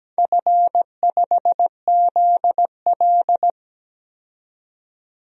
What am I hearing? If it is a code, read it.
Morse code, heard as a keyed tone, about 17 wpm, 700 Hz. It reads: F5ZL
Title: DXCC-100 Call Signs - 4 Characters - 17wpm